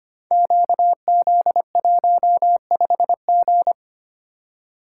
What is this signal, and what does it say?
Morse code, heard as a keyed tone, about 25 wpm, 700 Hz. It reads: QZ15G
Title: Code Groups - 5 Letter:Numbers - 25wpm